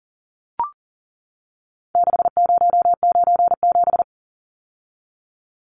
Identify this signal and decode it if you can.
Morse code, heard as a keyed tone, about 40 wpm, 700 Hz. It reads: 6097
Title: Numbers - 4-Digit 40wpm